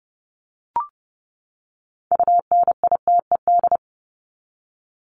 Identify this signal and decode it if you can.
Morse code, heard as a keyed tone, about 30 wpm, 700 Hz. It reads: UNITED
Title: Top 501-750 Words 30wpm